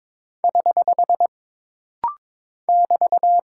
Morse code, heard as a keyed tone, about 22 wpm, 700 Hz. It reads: <HH> <BT>
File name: practice / 22wpm / Single Punctuation-Prosign - Rapid-Fire - 22wpm